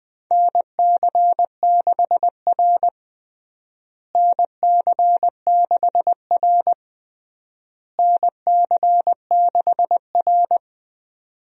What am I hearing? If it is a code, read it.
Morse code, heard as a keyed tone, about 20 wpm, 690 Hz. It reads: NC6R NC6R NC6R
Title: USA Call Signs - 4 Characters - Repeated 3x - 20wpm